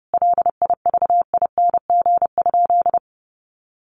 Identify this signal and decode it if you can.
Morse code, heard as a keyed tone, about 30 wpm, 700 Hz. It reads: LIVING?
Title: Sentences from Top 1100 Words 30wpm